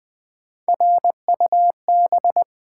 Morse code, heard as a keyed tone, about 20 wpm, 700 Hz. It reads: RUB
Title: Three Letter Words - 20wpm